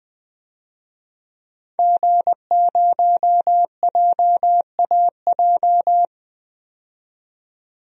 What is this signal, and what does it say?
Morse code, heard as a keyed tone, about 20 wpm, 700 Hz. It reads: G0JAJ
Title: DXCC-100 Call Signs - 5 Characters - 20wpm